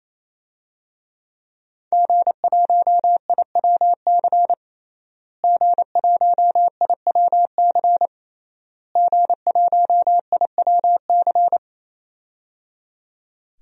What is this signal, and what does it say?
Morse code, heard as a keyed tone, about 28 wpm, 700 Hz. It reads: G1IWC G1IWC G1IWC
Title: DXCC-100 Call Signs - 5 Characters - Repeated 3x - 28wpm